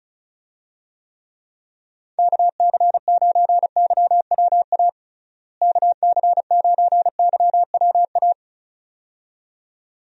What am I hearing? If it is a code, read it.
Morse code, heard as a keyed tone, about 36 wpm, 700 Hz. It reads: KC9YWA KC9YWA
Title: USA Call Signs - 6 Characters - Repeated 2x - 35wpm